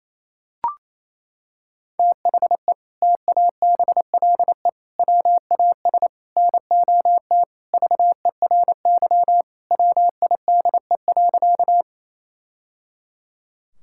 Morse code, heard as a keyed tone, about 28 wpm, 700 Hz. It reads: THE TABLE WAS NOT VERY WIDE.
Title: Sentences from Top 1000 Words - Review Part 6 - 28wpm